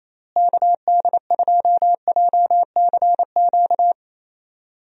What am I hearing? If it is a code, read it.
Morse code, heard as a keyed tone, about 28 wpm, 700 Hz. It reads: KD2JCQ
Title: USA Call Signs - 6 Characters - Repeated 3x - 28wpm